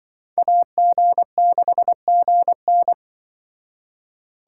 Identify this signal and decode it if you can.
Morse code, heard as a keyed tone, about 23 wpm, 700 Hz. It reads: AG6GN
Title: Speed Racing - USA 2x2 Call Signs 20wpm